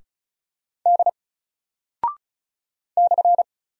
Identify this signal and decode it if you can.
Morse code, heard as a keyed tone, about 36 wpm, 700 Hz. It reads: D /
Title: Single Character (Letter:Number:Punctuation:Prosign) 35wpm